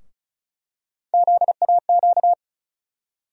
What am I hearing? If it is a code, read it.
Morse code, heard as a keyed tone, about 36 wpm, 700 Hz. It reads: ZAQ
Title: USA Call Sign Postfixes - 3 Letters 35wpm